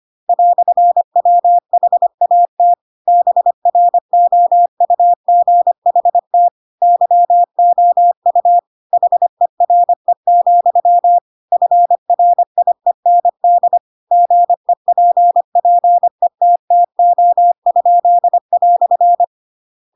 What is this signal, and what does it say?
Morse code, heard as a keyed tone, about 25 wpm, 700 Hz. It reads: "WHAT BROUGHT YOU HERE, FRIEND GEPPETTO?"
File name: Pinocchio / Pinocchio ch2 25 25 ik1hge